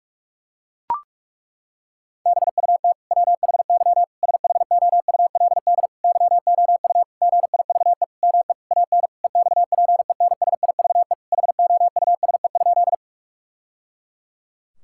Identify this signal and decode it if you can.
Morse code, heard as a keyed tone, about 41 wpm, 700 Hz. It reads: BUT WHY SHOULD YOU GIVE ME AN EXPENSIVE HOUSE?
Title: Sentences from Top 1000 Words - Review Part 6 - 45wpm